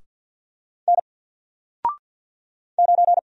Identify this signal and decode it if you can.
Morse code, heard as a keyed tone, about 48 wpm, 700 Hz. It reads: N 9
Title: Single Letter-Number Alt 50wpm